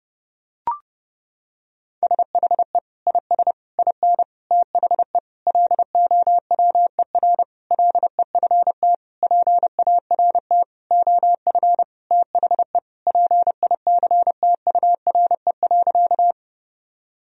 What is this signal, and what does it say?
Morse code, heard as a keyed tone, about 30 wpm, 700 Hz. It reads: SHE IS IN THE LOWER LEFT PART OF THE PICTURE.
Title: Sentences from Top 1000 Words - Review Part 4 - 30wpm